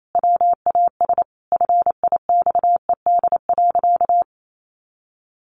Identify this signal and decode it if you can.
Morse code, heard as a keyed tone, about 28 wpm, 700 Hz. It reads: WAS FIXED.
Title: Sentences from Top 1100 Words 28wpm